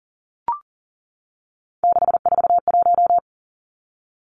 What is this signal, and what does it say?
Morse code, heard as a keyed tone, about 39 wpm, 700 Hz. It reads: 641
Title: Three Challenging Letter-Number Combinations 40wpm